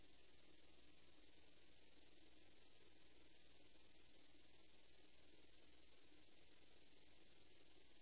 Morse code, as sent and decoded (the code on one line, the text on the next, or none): none